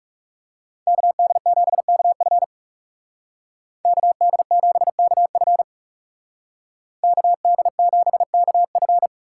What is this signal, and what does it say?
Morse code, heard as a keyed tone, about 37 wpm, 700 Hz. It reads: KD7KF KD7KF KD7KF
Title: Speed Racing - USA Call Signs 30wpm